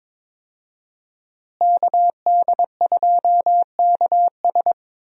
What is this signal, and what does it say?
Morse code, heard as a keyed tone, about 22 wpm, 700 Hz. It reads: KD2KS
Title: USA Call Signs - 5 Characters - Repeated 3x - 22wpm